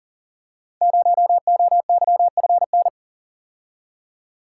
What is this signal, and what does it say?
Morse code, heard as a keyed tone, about 39 wpm, 700 Hz. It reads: 0OYFN
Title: Code Groups - 5 Letter:Numbers - 40wpm